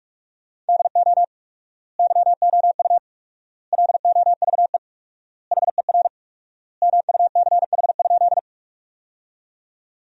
Morse code, heard as a keyed tone, about 46 wpm, 700 Hz. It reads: DO YOU LOVE HER MUCH?
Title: 4x Word-Spacing - Sentences from Top 400 Words - 45wpm